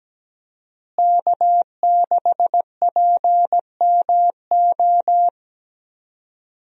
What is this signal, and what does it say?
Morse code, heard as a keyed tone, about 17 wpm, 700 Hz. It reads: K6PMO
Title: USA Call Signs - 5 Characters - Repeated 2x - 17wpm